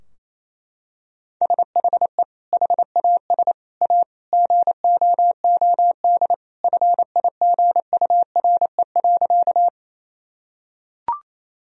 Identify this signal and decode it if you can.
Morse code, heard as a keyed tone, about 28 wpm, 700 Hz. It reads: SHE HAS A GOOD FIGURE.
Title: Sentences from Top 1000 Words - Review Part 2 - 28wpm